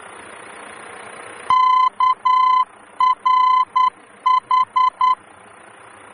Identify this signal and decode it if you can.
Morse code, heard as a keyed tone, about 10 wpm, 1020 Hz. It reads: KRH
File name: germany / mx / KRH-D-VOR